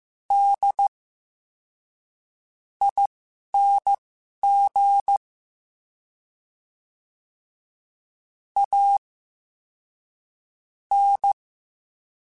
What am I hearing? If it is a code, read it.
Morse code, heard as a keyed tone, about 15 wpm, spaced as for 11 wpm, 800 Hz. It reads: D ING A N
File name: Special INGs-mp3s / Special INGs-mp3s / SlowToFaster ing @10